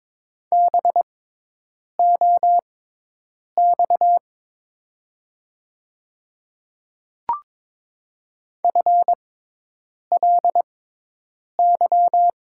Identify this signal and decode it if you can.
Morse code, heard as a keyed tone, about 22 wpm, spaced as for 8 wpm, 700 Hz. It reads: BOX FLY
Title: 6x Character Spacing - Top 500 Words - 22wpm